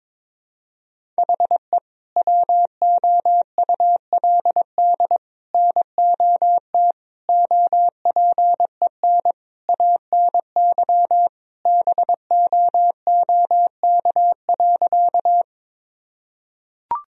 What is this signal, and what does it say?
Morse code, heard as a keyed tone, about 22 wpm, 700 Hz. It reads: HE WOULD NOT OPEN ANY BOOK.
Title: Sentences from Top 1000 Words - Review Part 1 - 22wpm